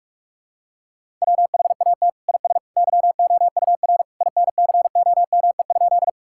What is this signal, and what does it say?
Morse code, heard as a keyed tone, about 41 wpm, 700 Hz. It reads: WHAT IS YOUR INCOME?
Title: Sentences from Top 1000 Words - Review Part 5 - 45wpm